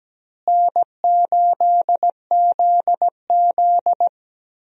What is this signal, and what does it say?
Morse code, heard as a keyed tone, about 17 wpm, 700 Hz. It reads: N8ZZ
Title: USA Call Signs - 4 Characters - Repeated 2x - 17wpm